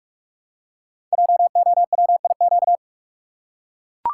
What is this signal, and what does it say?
Morse code, heard as a keyed tone, about 44 wpm, 690 Hz. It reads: JOWIQ
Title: Code Groups - 5 Letters - 45wpm